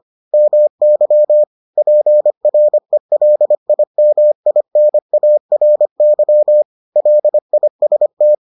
Morse code, heard as a keyed tone, about 25 wpm, 600 Hz. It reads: MY PRELIMINARY LIST